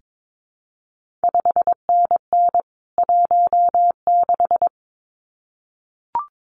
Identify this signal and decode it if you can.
Morse code, heard as a keyed tone, about 22 wpm, 700 Hz. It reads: 5NN 16
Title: CQ WW DX Contest Exchange - 22wpm